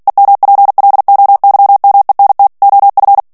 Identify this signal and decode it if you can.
Morse code, heard as a keyed tone, about 48 wpm, 800 Hz. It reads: EMPLOYMENT OF
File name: ANGLAIS texte clair 3 vitesse 45 mots minute